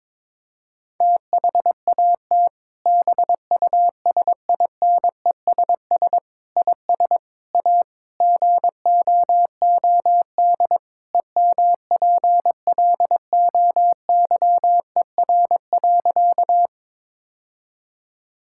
Morse code, heard as a keyed tone, about 22 wpm, 700 Hz. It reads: THAT BUSINESS IS A GOOD EMPLOYER.
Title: Sentences from Top 1600 Words 22wpm